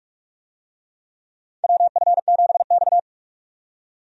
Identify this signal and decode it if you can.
Morse code, heard as a keyed tone, about 46 wpm, 700 Hz. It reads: WF7X
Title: USA Call Signs - 4 Characters - 45wpm